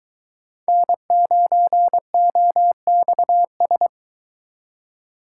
Morse code, heard as a keyed tone, about 23 wpm, 700 Hz. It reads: N9OXS